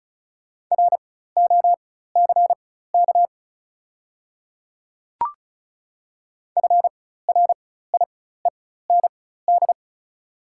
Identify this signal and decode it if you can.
Morse code, heard as a keyed tone, about 36 wpm, spaced as for 16 wpm, 700 Hz. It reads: ROCK FRIEND
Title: 4x Character Spacing - Top 500 Words - 35wpm